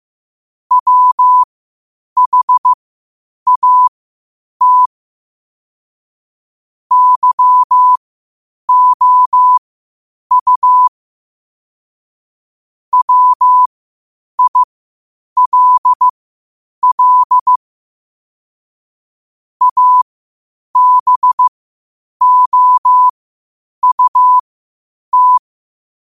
Morse code, heard as a keyed tone, about 15 wpm, spaced as for 8 wpm, 1000 Hz. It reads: WHAT YOU WILL ABOUT
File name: qotdfiles / 8wpm